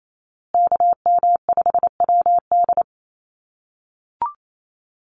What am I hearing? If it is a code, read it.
Morse code, heard as a keyed tone, about 28 wpm, 700 Hz. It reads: KM5WD
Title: USA Call Signs Repeated 2x 28wpm